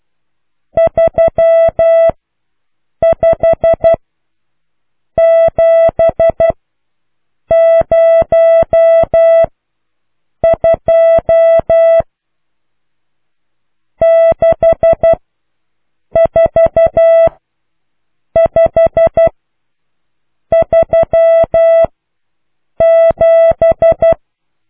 Morse code, heard as a keyed tone, about 12 wpm, spaced as for 6 wpm, 660 Hz. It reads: 35702 64537